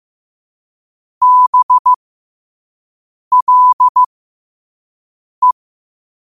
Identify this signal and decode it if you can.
Morse code, heard as a keyed tone, about 15 wpm, spaced as for 5 wpm, 1000 Hz. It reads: BLE
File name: qotdfiles / 5wpm